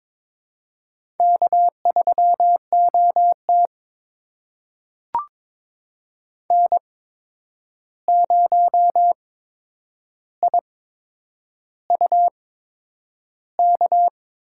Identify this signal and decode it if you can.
Morse code, heard as a keyed tone, about 22 wpm, spaced as for 9 wpm, 700 Hz. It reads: K3OT N0IUK
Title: USA Call Signs - 8x Character Spacing - 22wpm